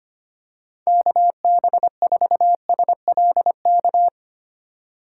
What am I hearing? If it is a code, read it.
Morse code, heard as a keyed tone, about 25 wpm, 700 Hz. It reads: KB4SLK
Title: USA Call Signs - 6 Characters - 6x Character Spacing - 25wpm